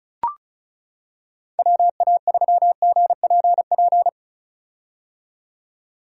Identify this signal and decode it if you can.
Morse code, heard as a keyed tone, about 34 wpm, 700 Hz. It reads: WA3GPP
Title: K1USN SST QSO Practice - 35wpm